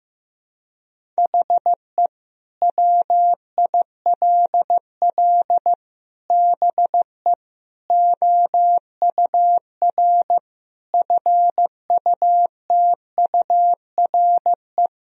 Morse code, heard as a keyed tone, about 15 wpm, 700 Hz. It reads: HE WILL BE OUR FUTURE
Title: Sentences from Top 1000 Words - Review Part 3 - 15wpm